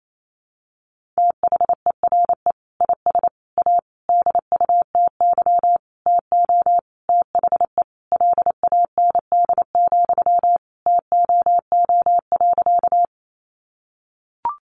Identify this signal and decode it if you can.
Morse code, heard as a keyed tone, about 28 wpm, 700 Hz. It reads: THERE IS A DUTY TO THE LAND, TOO.
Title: Sentences from Top 1500 Words 28wpm